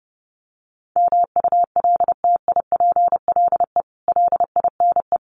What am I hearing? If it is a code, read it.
Morse code, heard as a keyed tone, about 30 wpm, 700 Hz. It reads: MULTIPLE LINE
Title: Sets of 5 Words 30wpm